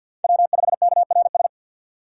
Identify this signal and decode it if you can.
Morse code, heard as a keyed tone, about 53 wpm, 700 Hz. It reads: W5KRS
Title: USA Call Signs - 5 Characters - 4x Character Spacing - 50wpm